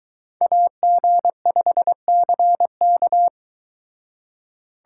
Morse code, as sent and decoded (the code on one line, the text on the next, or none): .- --. ..... -.-. -.-
AG5CK